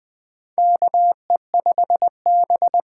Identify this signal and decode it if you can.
Morse code, heard as a keyed tone, about 21 wpm, 700 Hz. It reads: KE5B
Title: USA Call Signs - 6 Characters - 6x Character Spacing - 20wpm